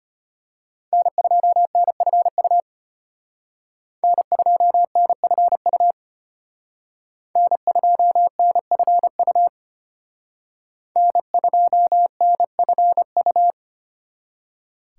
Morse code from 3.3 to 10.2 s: N2NFU N2NFU

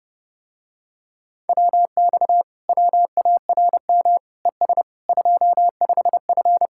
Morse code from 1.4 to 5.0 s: WX WARM ES

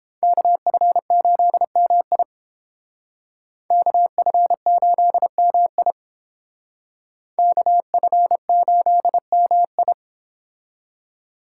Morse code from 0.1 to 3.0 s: KF8MI

